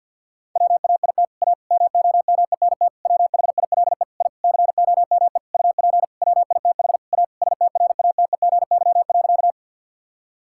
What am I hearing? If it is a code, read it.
Morse code, heard as a keyed tone, about 47 wpm, 690 Hz. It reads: WAIT A MOMENT WHILE I COME UP WITH A STRATEGY.
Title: Sentences from Top 1000 Words - Review Part 5 - 50wpm